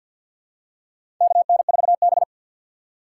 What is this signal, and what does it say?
Morse code, heard as a keyed tone, about 48 wpm, 700 Hz. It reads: KN4B
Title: USA Call Signs - 3x Character Spacing - 50wpm